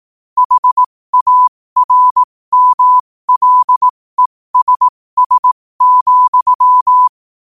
Morse code, heard as a keyed tone, about 18 wpm, spaced as for 15 wpm, 1000 Hz. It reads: HARMLESS,
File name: qotdfiles / 15wpm